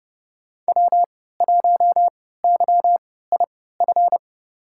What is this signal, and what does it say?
Morse code, heard as a keyed tone, about 30 wpm, spaced as for 17 wpm, 700 Hz. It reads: W1YIF
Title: USA Call Signs - 5 Characters - 3x Character Spacing - 30wpm